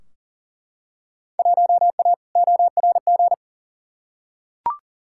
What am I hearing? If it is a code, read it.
Morse code, heard as a keyed tone, about 40 wpm, 700 Hz. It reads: 1A ORG
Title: Field Day Exchanges - 40wpm